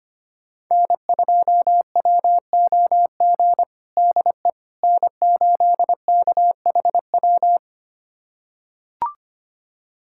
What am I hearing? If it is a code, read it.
Morse code, heard as a keyed tone, about 25 wpm, 700 Hz. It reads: N2WOG DE N8KHW
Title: USA Callsign DE Callsign - 5 Characters - 25wpm